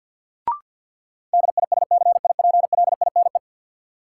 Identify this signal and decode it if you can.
Morse code, heard as a keyed tone, about 53 wpm, 700 Hz. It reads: DISCIPLINE
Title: Sentences from Top 900 Words - New Word Drill - 50wpm